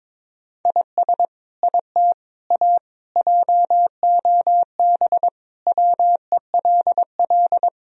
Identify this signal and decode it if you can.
Morse code, heard as a keyed tone, about 21 wpm, 700 Hz. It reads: IS IT A JOB WELL
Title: Sentences from Top 1000 Words - Review Part 3 - 22wpm